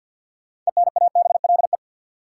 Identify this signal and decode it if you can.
Morse code, heard as a keyed tone, about 52 wpm, 700 Hz. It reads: ENABLE
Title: Sentences from Top 1500 Words - New Word Drill Alt - 50wpm